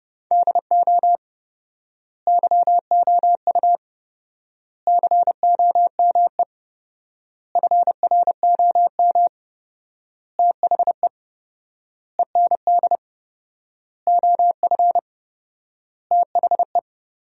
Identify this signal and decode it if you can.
Morse code, heard as a keyed tone, about 29 wpm, 700 Hz. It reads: DO YOU COME FROM THE END OF THE
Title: 4x Word-Spacing - Sentences from Top 200 Words - 30wpm